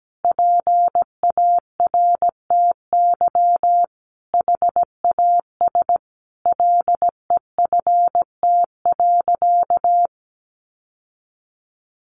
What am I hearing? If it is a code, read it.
Morse code, heard as a keyed tone, about 17 wpm, 700 Hz. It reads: PARTY HAS LEFT.
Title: Sentences from Top 1500 Words 17wpm